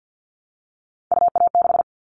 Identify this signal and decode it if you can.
Morse code, heard as a keyed tone, about 50 wpm, 700 Hz. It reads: UA6
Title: Callsign Prefixes for Top 50 DXCC - 50wpm